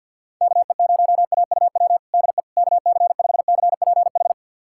Morse code, heard as a keyed tone, about 53 wpm, 700 Hz. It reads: KE0AUW DE KC5CPH
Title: USA Callsign DE Callsign - 6 Characters - 50wpm